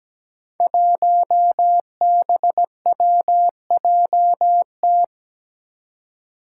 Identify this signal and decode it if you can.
Morse code, heard as a keyed tone, about 17 wpm, 700 Hz. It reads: 1BWJT
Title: Five Challenging Letter-Number Combinations 17wpm